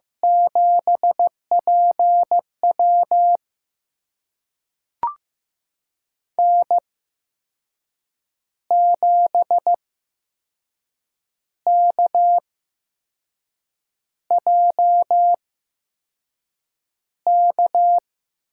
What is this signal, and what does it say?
Morse code, heard as a keyed tone, about 15 wpm, spaced as for 5 wpm, 700 Hz. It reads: 7PW N7KJK
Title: USA Call Signs - 5 Characters - 8x Character Spacing - 15wpm